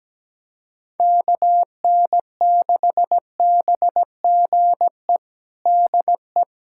Morse code, heard as a keyed tone, about 17 wpm, 700 Hz. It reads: KN6BGE DE
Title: USA Callsign DE Callsign 17wpm